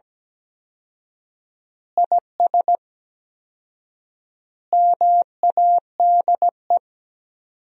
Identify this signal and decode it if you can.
Morse code, heard as a keyed tone, about 17 wpm, 700 Hz. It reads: IS MADE
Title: 4x Word-Spacing - Sentences from Top 500 Words - 17wpm